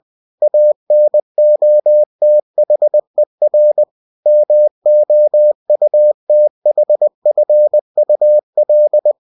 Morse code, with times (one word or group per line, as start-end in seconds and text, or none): 0.4-3.8 ANOTHER
4.3-9.1 MOUTHFUL